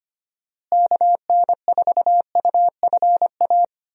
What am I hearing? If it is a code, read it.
Morse code, heard as a keyed tone, about 25 wpm, 700 Hz. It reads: KN4UFA
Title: USA Call Signs - 6 Characters - Repeated 3x - 25wpm